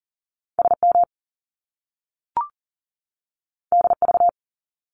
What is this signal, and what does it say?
Morse code, heard as a keyed tone, about 41 wpm, 700 Hz. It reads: SM DV
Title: Top Two Letter Combinations in Words 40wpm